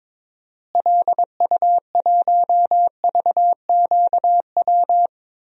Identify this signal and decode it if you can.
Morse code, heard as a keyed tone, about 22 wpm, 700 Hz. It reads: LU1VQW
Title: DXCC-100 Call Signs - 6 Characters - Repeated 2x - 22wpm